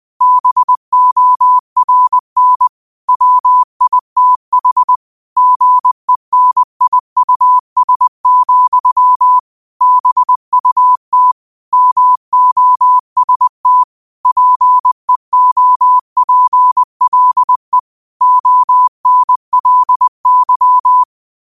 Morse code, heard as a keyed tone, about 20 wpm, 1000 Hz. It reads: BORN WITH GENIUS, BUT MOST PEOPLE ONLY
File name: qotdfiles / 20wpm